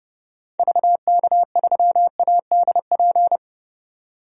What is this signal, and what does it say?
Morse code, heard as a keyed tone, about 32 wpm, 700 Hz. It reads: VK3ADP
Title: DXCC-100 Call Signs - 6 Characters - 30wpm